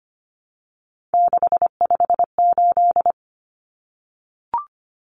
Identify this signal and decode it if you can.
Morse code, heard as a keyed tone, about 25 wpm, 700 Hz. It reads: 658
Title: Numbers - 3-Digit 25wpm